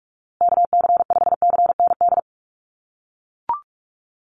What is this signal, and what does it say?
Morse code, heard as a keyed tone, about 44 wpm, 700 Hz. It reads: KC5CND
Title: USA Call Signs Repeated 2x 45wpm